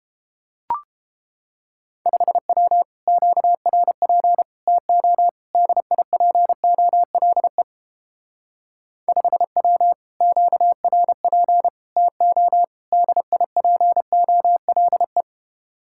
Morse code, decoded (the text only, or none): 5W QRP TO DIPOLE 5W QRP TO DIPOLE